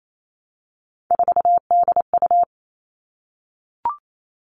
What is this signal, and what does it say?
Morse code, heard as a keyed tone, about 28 wpm, 700 Hz. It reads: 4DU
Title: All Cut Numbers - 3-Digit 28wpm